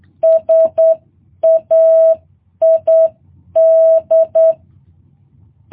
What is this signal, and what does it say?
Morse code, heard as a keyed tone, about 8 wpm, 650 Hz. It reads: SAID
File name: jh8cbh